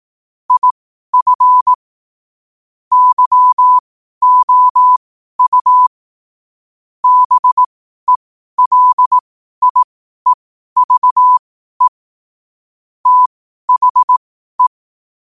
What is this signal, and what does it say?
Morse code, heard as a keyed tone, about 18 wpm, spaced as for 12 wpm, 1000 Hz. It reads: IF YOU BELIEVE THE
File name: qotdfiles / 12wpm